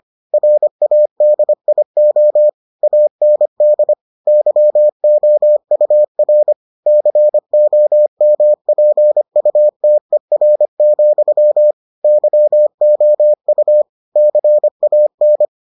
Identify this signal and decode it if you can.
Morse code, heard as a keyed tone, about 25 wpm, 600 Hz. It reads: RADIO AND YOUR COMPUTER, YOU CAN